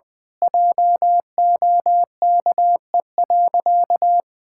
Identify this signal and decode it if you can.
Morse code, heard as a keyed tone, about 20 wpm, 700 Hz. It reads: JOKE.
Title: Pangram Challenge v2 - 20wpm